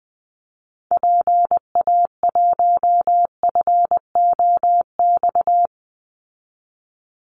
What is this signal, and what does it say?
Morse code, heard as a keyed tone, about 19 wpm, 700 Hz. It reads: PA1FOX
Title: Call Signs Repeated 2x - 20wpm